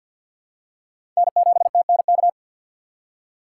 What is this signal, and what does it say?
Morse code, heard as a keyed tone, about 49 wpm, 700 Hz. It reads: N7TNK
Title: USA Call Signs - 5 Characters - 2x Character Spacing - 50wpm